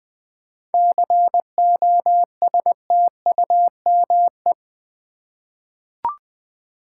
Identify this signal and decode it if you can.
Morse code, heard as a keyed tone, about 20 wpm, 700 Hz. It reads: COSTUME
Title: 7 Letter Words - First 3 Letters Given - 20wpm